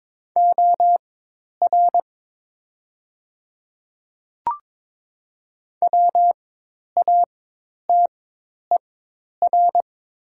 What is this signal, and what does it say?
Morse code, heard as a keyed tone, about 22 wpm, spaced as for 10 wpm, 700 Hz. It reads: OR WATER
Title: 4x Character Spacing - Top 100 Words - 22wpm